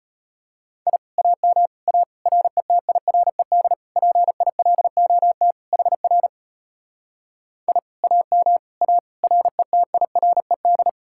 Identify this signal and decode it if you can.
Morse code, heard as a keyed tone, about 33 wpm, 700 Hz. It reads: I AM A RETIRED PILOT HR I AM A RETIRED